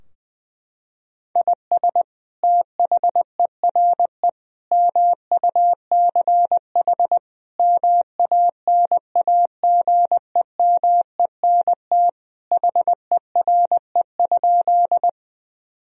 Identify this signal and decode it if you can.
Morse code, heard as a keyed tone, about 21 wpm, 700 Hz. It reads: IS THERE MUCH MANAGEMENT HERE?
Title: Sentences from Top 1000 Words - Review Part 4 - 20wpm